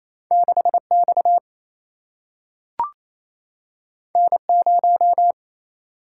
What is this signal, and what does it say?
Morse code, heard as a keyed tone, about 28 wpm, 700 Hz. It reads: 6 X N 0
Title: Single-Letter Single-Number 28wpm